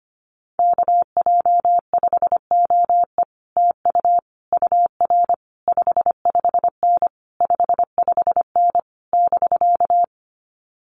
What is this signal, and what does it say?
Morse code, heard as a keyed tone, about 25 wpm, 700 Hz. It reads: KJ5OE TU UR 55N 55N <BK>
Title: Common QSO Phrases Encore 25wpm